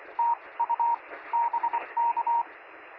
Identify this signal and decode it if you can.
Morse code, heard as a keyed tone, about 25 wpm, spaced as for 20 wpm, 940 Hz, with radio noise. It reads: TU BK
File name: sm7lcb / 0423z yl2aj 23cm